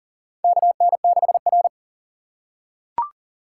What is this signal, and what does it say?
Morse code, heard as a keyed tone, about 37 wpm, 700 Hz. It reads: KN6R
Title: USA Call Signs - 6x Character Spacing - 40wpm